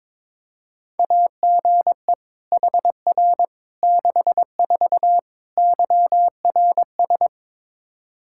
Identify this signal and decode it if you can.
Morse code, heard as a keyed tone, about 22 wpm, 700 Hz. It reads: AGE HR 64 YRS
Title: K1USN SST QSO Practice 22wpm